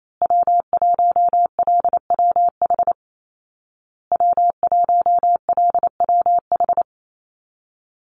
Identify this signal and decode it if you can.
Morse code, heard as a keyed tone, about 28 wpm, 700 Hz. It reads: W1LWH W1LWH
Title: Call Signs Repeated 2x - 28wpm